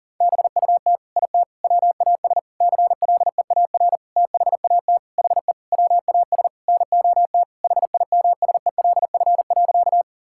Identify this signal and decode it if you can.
Morse code, heard as a keyed tone, about 40 wpm, 700 Hz. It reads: BUT IT WAS CLEAR THAT HE WAS NOT HIMSELF.
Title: Sentences from Top 1600 Words 40wpm